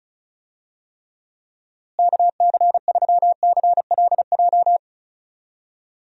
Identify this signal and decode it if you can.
Morse code, heard as a keyed tone, about 34 wpm, 700 Hz. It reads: KC3CLJ
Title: USA Call Signs - 6 Characters - Repeated 3x - 35wpm